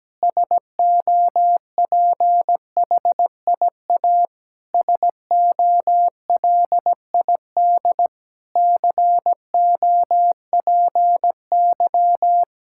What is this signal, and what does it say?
Morse code, heard as a keyed tone, about 17 wpm, 700 Hz. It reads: SOPHIA SOLID COPY
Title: Practice QSO - Exchange 3 - More Info - 17wpm